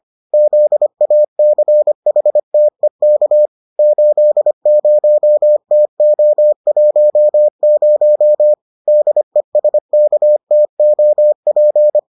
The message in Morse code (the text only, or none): ZACHTEK 80TO10 DESKTOP